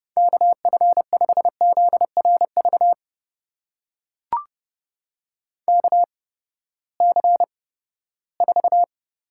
Morse code, heard as a keyed tone, about 30 wpm, spaced as for 17 wpm, 700 Hz. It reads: KF5ZRV KC4
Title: USA Call Signs - 6 Characters - 8x Character Spacing - 30wpm